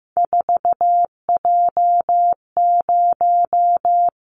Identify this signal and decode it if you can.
Morse code, heard as a keyed tone, about 15 wpm, 700 Hz. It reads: 4J0
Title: Callsign Prefixes for Top 100 DXCC - 15wpm